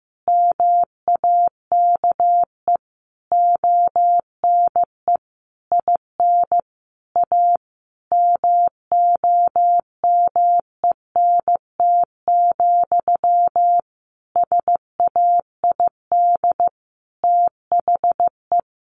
Morse code, heard as a keyed tone, about 15 wpm, 700 Hz. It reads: MAKE ONE IN A MOMENT, SAID THE
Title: Sentences from Top 1000 Words 15wpm